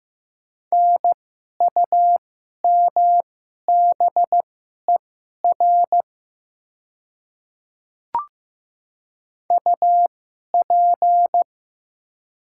Morse code, heard as a keyed tone, about 15 wpm, spaced as for 11 wpm, 690 Hz. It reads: NUMBER UP
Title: 2x Character Spacing - Top 100 Words - 15wpm